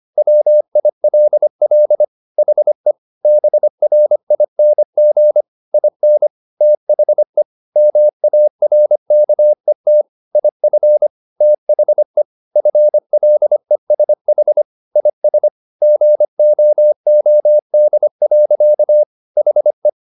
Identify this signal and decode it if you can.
Morse code, heard as a keyed tone, about 25 wpm, 600 Hz. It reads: WILL HE BRING IN THE MARKET IF THE FLESH IS GOOD. HE
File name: the old man and the sea part 1 25wpm